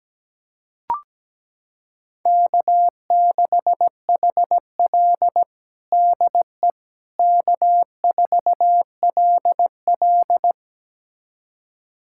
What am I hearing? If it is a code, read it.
Morse code, heard as a keyed tone, about 17 wpm, 700 Hz. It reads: K6HL DE K4LL
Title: USA Callsign DE Callsign - 4 Characters - 17wpm